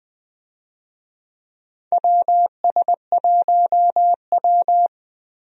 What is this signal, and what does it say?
Morse code, heard as a keyed tone, about 20 wpm, 700 Hz. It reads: WS1W
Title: USA Call Signs - 4 Characters - Repeated 2x - 20wpm